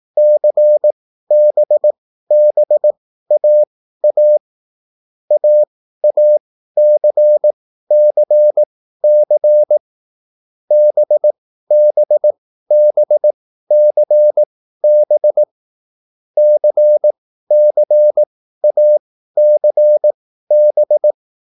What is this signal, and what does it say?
Morse code, heard as a keyed tone, about 18 wpm, spaced as for 13 wpm, 600 Hz. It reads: CBBAA AACCC BBBCB CCACB
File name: QSL / CW lesson